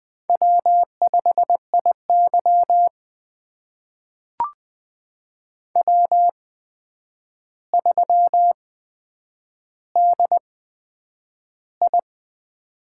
Morse code, heard as a keyed tone, about 20 wpm, spaced as for 9 wpm, 700 Hz. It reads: W5IY W3DI
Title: USA Call Signs - 4 Characters - 8x Character Spacing - 20wpm